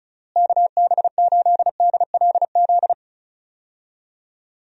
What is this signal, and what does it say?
Morse code, heard as a keyed tone, about 36 wpm, 700 Hz. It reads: KB8DLZ
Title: USA Call Signs - 6 Characters - Repeated 3x - 35wpm